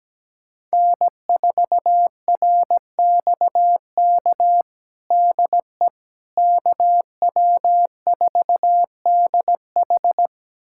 Morse code, heard as a keyed tone, about 17 wpm, 700 Hz. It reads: N4RXK DE KW4DH
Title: USA Callsign DE Callsign - 5 Characters - 17wpm